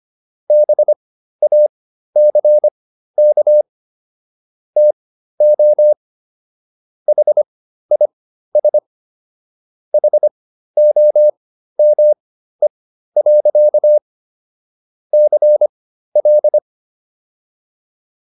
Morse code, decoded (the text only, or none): BACK TO HIS HOME. CL